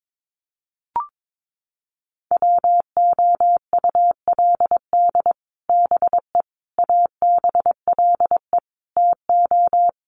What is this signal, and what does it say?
Morse code, heard as a keyed tone, about 22 wpm, 700 Hz. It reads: WOULD BE ABLE TO
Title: Sets of 4 Words 22wpm